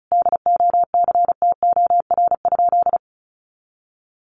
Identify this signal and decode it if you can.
Morse code, heard as a keyed tone, about 34 wpm, 700 Hz. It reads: DOCTOR?
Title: Sentences from Top 1500 Words 35wpm